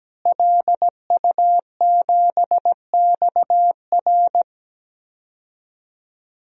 Morse code, heard as a keyed tone, about 17 wpm, 700 Hz. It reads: LU7XR
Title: DXCC-100 Call Signs - 5 Characters - Repeated 3x - 17wpm